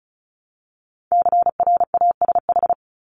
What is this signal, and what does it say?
Morse code, heard as a keyed tone, about 34 wpm, 700 Hz. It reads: CRASH
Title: Top 1751-2000 Words 35wpm